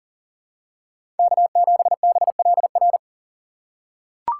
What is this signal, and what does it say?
Morse code, heard as a keyed tone, about 40 wpm, 700 Hz. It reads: K7BLR
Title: USA Call Signs - 5 Characters - 3x Character Spacing - 40wpm